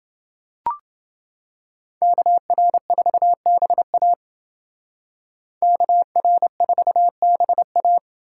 Morse code, heard as a keyed tone, about 29 wpm, 700 Hz. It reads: KR4BA KR4BA